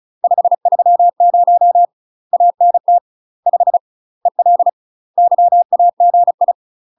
Morse code, heard as a keyed tone, about 34 wpm, 700 Hz. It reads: 530 ANT 5 EL YAGI
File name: UK MASTER QSOs TRAINING / 35WPM / UK MASTER QSOs 0024